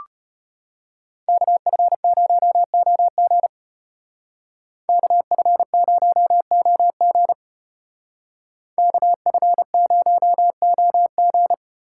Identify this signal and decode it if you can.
Morse code, heard as a keyed tone, about 33 wpm, 700 Hz. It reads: KF0OG KF0OG KF0OG